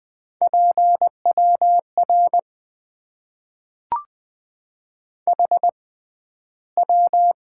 Morse code, heard as a keyed tone, about 20 wpm, spaced as for 12 wpm, 700 Hz. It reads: PWR HW
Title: Top 90 QSO Elements - 6x Character Spacing - 20wpm